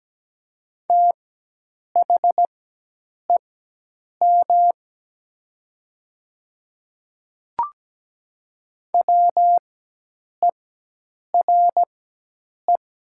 Morse code, heard as a keyed tone, about 17 wpm, spaced as for 8 wpm, 700 Hz. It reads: THEM WERE
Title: 4x Character Spacing - Top 100 Words - 17wpm